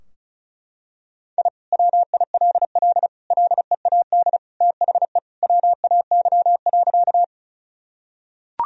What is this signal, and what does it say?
Morse code, heard as a keyed tone, about 34 wpm, 700 Hz. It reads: I WILL LEAD THE WAY.
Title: Sentences from Top 1000 Words - Review Part 2 - 35wpm